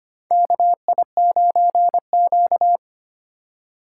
Morse code, heard as a keyed tone, about 25 wpm, 700 Hz. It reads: KI9Q
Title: USA Call Signs - 4 Characters - 4x Character Spacing - 25wpm